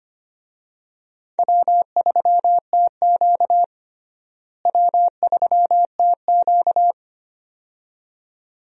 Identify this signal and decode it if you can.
Morse code, heard as a keyed tone, about 25 wpm, 700 Hz. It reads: W3TQ W3TQ
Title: USA Call Signs - 4 Characters - Repeated 2x - 25wpm